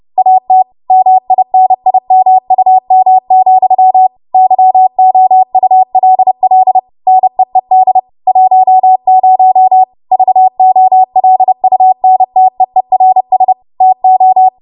Morse code, heard as a keyed tone, about 30 wpm, 750 Hz. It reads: AT MINIMUM, YOULL NEED 10 VOLUNTEERS TO
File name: Morse / 30WPM